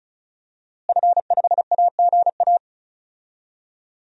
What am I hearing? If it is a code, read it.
Morse code, heard as a keyed tone, about 34 wpm, 700 Hz. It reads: F5AGA
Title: DXCC-100 Call Signs - 5 Characters - Repeated 2x - 35wpm